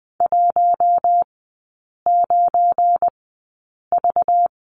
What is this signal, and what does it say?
Morse code, heard as a keyed tone, about 19 wpm, spaced as for 8 wpm, 690 Hz. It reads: 19V
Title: 205 Lesson 33e - Letter j - Three Characters 20wpm